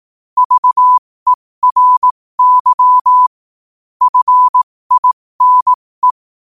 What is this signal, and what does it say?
Morse code, heard as a keyed tone, about 18 wpm, spaced as for 15 wpm, 1000 Hz. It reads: VERY FINE
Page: New quotes every day in morse code at 15 Words per minute.